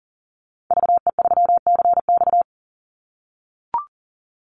Morse code, heard as a keyed tone, about 39 wpm, 700 Hz. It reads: VE3CX
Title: Call Signs - 40wpm